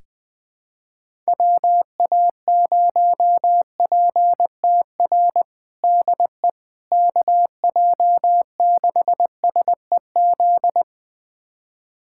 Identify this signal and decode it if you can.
Morse code, heard as a keyed tone, about 20 wpm, 700 Hz. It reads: WA0PTR DE KJ6SEZ
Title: USA Callsign DE Callsign - 6 Characters - 20wpm